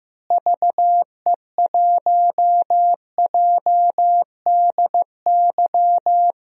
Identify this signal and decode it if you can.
Morse code, heard as a keyed tone, about 15 wpm, 700 Hz. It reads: VE1JDY
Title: DXCC-100 Call Signs - 6 Characters - Repeated 2x - 15wpm